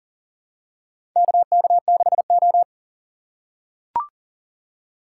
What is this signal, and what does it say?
Morse code, heard as a keyed tone, about 40 wpm, 700 Hz. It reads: KK6O
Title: USA Call Signs - 4 Characters - 4x Character Spacing - 40wpm